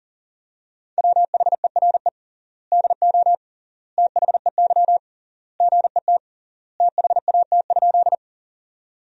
Morse code, heard as a keyed tone, about 40 wpm, 700 Hz. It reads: WHERE DO THEY GET THAT?
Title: 3x Word-Spacing - Sentences from Top 200 Words - 40wpm